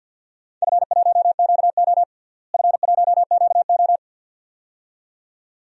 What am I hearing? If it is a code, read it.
Morse code, heard as a keyed tone, about 53 wpm, 700 Hz. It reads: F1QO F1QO
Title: DXCC-100 Call Signs - 4 Characters - Repeated 2x - 50wpm